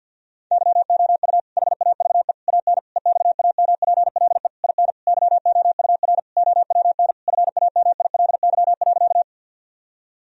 Morse code, heard as a keyed tone, about 52 wpm, 700 Hz. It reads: YOU HAVE AN EXAMPLE IN YOUR OWN FAMILY.
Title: Sentences from Top 1000 Words - Review Part 1 - 50wpm